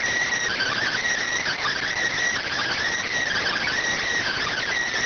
Emergency Alert System (SAME) tones (header bursts), heard as harsh buzzing bursts. It is heard from 0.5 to 4.7 s.